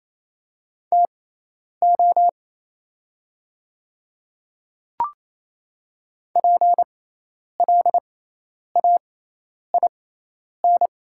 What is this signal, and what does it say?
Morse code, heard as a keyed tone, about 28 wpm, spaced as for 10 wpm, 700 Hz. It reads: TO PLAIN
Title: 6x Character Spacing - Top 500 Words - 28wpm